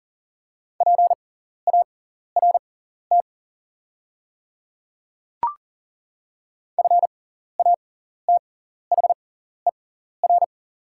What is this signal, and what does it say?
Morse code, heard as a keyed tone, about 40 wpm, spaced as for 14 wpm, 700 Hz. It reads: PART FATHER
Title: 6x Character Spacing - Top 300 Words - 40wpm